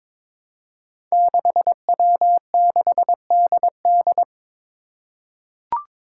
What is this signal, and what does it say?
Morse code, heard as a keyed tone, about 22 wpm, 700 Hz. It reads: 6W6DD
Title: Five Challenging Letter-Number Combinations 22wpm